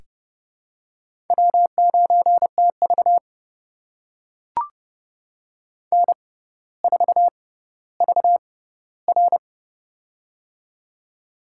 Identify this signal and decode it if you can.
Morse code, heard as a keyed tone, about 30 wpm, spaced as for 15 wpm, 700 Hz. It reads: W9TV N4VR
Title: USA Call Signs - 4 Characters - 6x Character Spacing - 30wpm